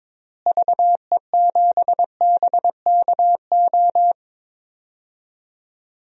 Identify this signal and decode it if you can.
Morse code, heard as a keyed tone, about 22 wpm, 700 Hz. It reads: VE7BKO
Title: DXCC-100 Call Signs - 6 Characters - Repeated 3x - 22wpm